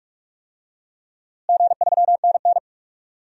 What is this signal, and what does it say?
Morse code, heard as a keyed tone, about 43 wpm, 700 Hz. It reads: G3NN
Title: DXCC-100 Call Signs - 4 Characters - Repeated 2x - 45wpm